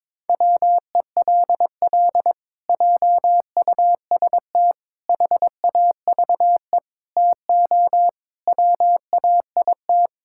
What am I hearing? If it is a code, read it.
Morse code, heard as a keyed tone, about 21 wpm, 700 Hz. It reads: WELL JUST HAVE TO WAIT
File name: practice / 22wpm / Sets of 7 Words - 22wpm